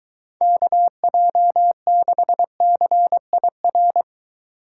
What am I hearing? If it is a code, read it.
Morse code, heard as a keyed tone, about 23 wpm, 700 Hz. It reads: KJ6CIR